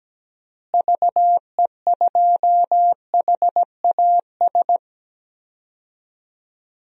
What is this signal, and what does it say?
Morse code, heard as a keyed tone, about 17 wpm, 700 Hz. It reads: VE2HAS
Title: DXCC-100 Call Signs - 6 Characters - Repeated 3x - 17wpm